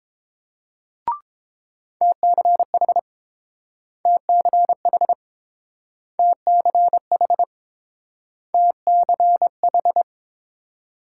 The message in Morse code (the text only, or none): TCH TCH TCH TCH